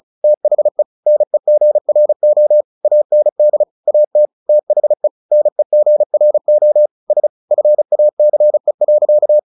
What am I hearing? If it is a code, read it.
Morse code, heard as a keyed tone, about 34 wpm, 600 Hz. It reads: THE NEGRO AND AT THE NEGRO S FACE.
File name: the old man and the sea 2 35wpm